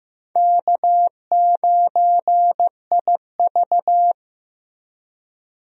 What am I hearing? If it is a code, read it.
Morse code, heard as a keyed tone, about 15 wpm, 700 Hz. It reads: K9IV
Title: USA Call Signs - 4 Characters - Repeated 3x - 15wpm